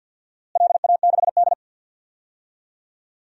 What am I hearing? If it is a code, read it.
Morse code, heard as a keyed tone, about 50 wpm, 700 Hz. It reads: LA6D
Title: DXCC-100 Call Signs - 4 Characters - Repeated 2x - 50wpm